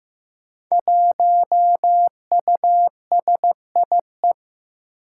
Morse code, heard as a keyed tone, about 15 wpm, 700 Hz. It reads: 1USIE
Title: Five Challenging Letter-Number Combinations 15wpm